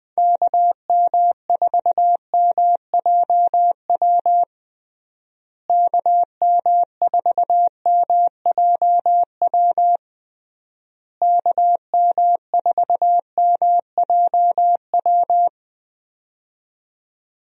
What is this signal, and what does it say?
Morse code, heard as a keyed tone, about 20 wpm, 700 Hz. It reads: KM4MJW KM4MJW KM4MJW
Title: USA Call Signs - 6 Characters - Repeated 3x - 20wpm